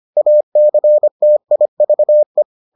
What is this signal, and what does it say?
Morse code, heard as a keyed tone, about 25 wpm, 600 Hz. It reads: ACTIVE